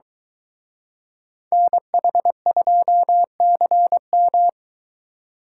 Morse code, heard as a keyed tone, about 23 wpm, 700 Hz. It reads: NH2CM